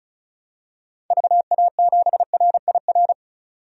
Morse code, heard as a keyed tone, about 34 wpm, 700 Hz. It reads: VA7RIR
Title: DXCC-100 Call Signs - 6 Characters - 35wpm